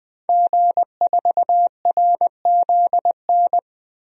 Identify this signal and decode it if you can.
Morse code, heard as a keyed tone, about 21 wpm, 700 Hz. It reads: G4RZN
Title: DXCC-100 Call Signs - 5 Characters - 20wpm